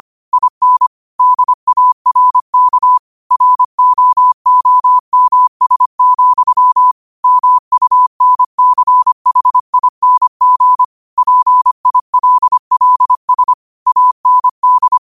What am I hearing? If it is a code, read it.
Morse code, heard as a keyed tone, about 25 wpm, 1000 Hz. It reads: IN DARK ROOMS, MUNCHING PILLS AND
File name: qotdfiles / 25wpm